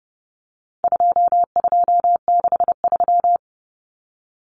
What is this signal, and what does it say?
Morse code, heard as a keyed tone, about 30 wpm, 700 Hz. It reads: 2263
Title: Common Cut Numbers - 4-Digit 30wpm